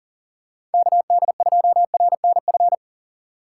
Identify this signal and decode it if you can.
Morse code, heard as a keyed tone, about 40 wpm, 700 Hz. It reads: KD2RNF
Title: USA Call Signs - 6 Characters - 3x Character Spacing - 40wpm